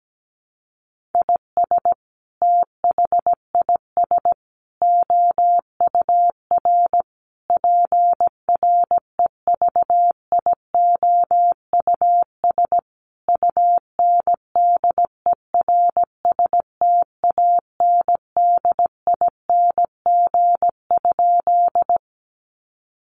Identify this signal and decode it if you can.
Morse code, heard as a keyed tone, about 17 wpm, 700 Hz. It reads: IS THIS OUR PREVIOUS UNDERSTANDING?
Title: Sentences from Top 1500 Words 17wpm